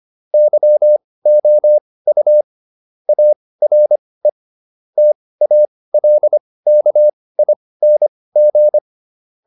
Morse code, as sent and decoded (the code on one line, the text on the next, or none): -.-- --- ..- / .- .-. / - .- .-.. -.- .. -. --.
YOU ARE TALKING